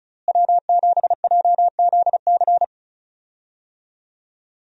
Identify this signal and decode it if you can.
Morse code, heard as a keyed tone, about 34 wpm, 700 Hz. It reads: W7JZC
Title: USA Call Signs - 5 Characters - Repeated 2x - 35wpm